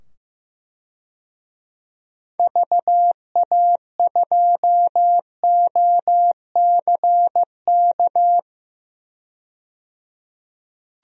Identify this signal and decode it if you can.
Morse code, heard as a keyed tone, about 15 wpm, 700 Hz. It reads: VA2OCK
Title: DXCC-100 Call Signs - 6 Characters - 15wpm